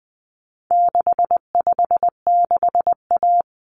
Morse code, heard as a keyed tone, about 20 wpm, 700 Hz. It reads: 656A
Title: Common Cut Numbers - 4-Digit 20wpm